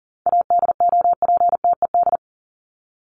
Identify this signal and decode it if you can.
Morse code, heard as a keyed tone, about 40 wpm, 700 Hz. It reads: ADOPTED
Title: ED Suffix - Top 100 Words with Base Words 40wpm